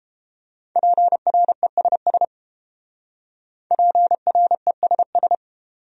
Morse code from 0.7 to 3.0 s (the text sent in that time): PRESS